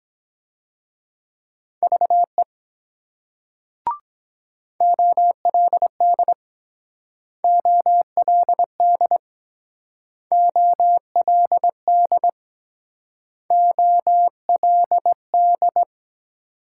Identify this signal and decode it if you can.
Morse code, heard as a keyed tone, about 22 wpm, 700 Hz. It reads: VE OLD OLD OLD OLD